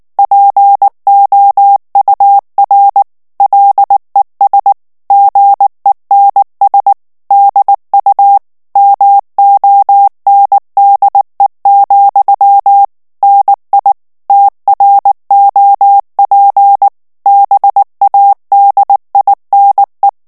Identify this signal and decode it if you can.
Morse code, heard as a keyed tone, about 19 wpm, 800 Hz. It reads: POUR LES GENS DU MONDE, NI TROP BADINE
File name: FRANCAIS TEXTE CLAIR 2 VITESSE 18 MOTS MINUTES